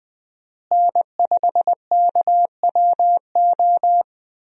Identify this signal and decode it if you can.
Morse code, heard as a keyed tone, about 20 wpm, 700 Hz. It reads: N5KWO
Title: USA Call Signs - 5 Characters - Repeated 3x - 20wpm